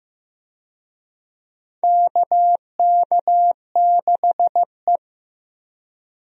Morse code, heard as a keyed tone, about 14 wpm, 700 Hz. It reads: KK6E
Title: USA Call Signs - 4 Characters - Repeated 3x - 15wpm